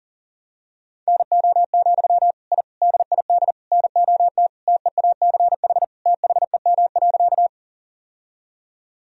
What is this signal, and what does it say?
Morse code, heard as a keyed tone, about 37 wpm, 700 Hz. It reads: NO, I DID NOT TEACH THEM.
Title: Sentences from Top 1000 Words - Review Part 2 - 40wpm